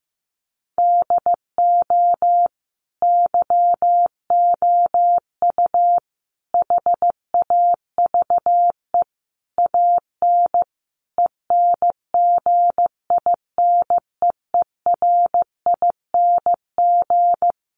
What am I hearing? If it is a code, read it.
Morse code, heard as a keyed tone, about 14 wpm, 700 Hz. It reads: DO YOU HAVE AN ENGINEERING
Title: Sentences from Top 1500 Words 15wpm